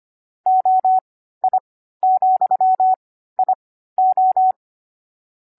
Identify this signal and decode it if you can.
Morse code, heard as a keyed tone, about 25 wpm, spaced as for 14 wpm, 750 Hz. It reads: OI,IO
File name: CW-Koch / koch.29.3.groups5.15wpm